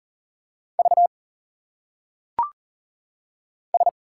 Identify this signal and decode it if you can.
Morse code, heard as a keyed tone, about 40 wpm, 700 Hz. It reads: V S
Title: Single Letters - Phonetic Pronunciation 40wpm